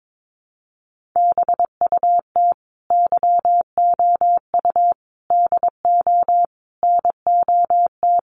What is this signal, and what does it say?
Morse code, heard as a keyed tone, about 22 wpm, 700 Hz. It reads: BUT YOU DO NOT
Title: Sentences from Top 600 Words 22wpm